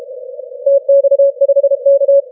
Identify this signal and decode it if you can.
Morse code, heard as a keyed tone, about 32 wpm, 560 Hz. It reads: TX5K
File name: wp / tx5k 30m cw 08 mar 2013 0929z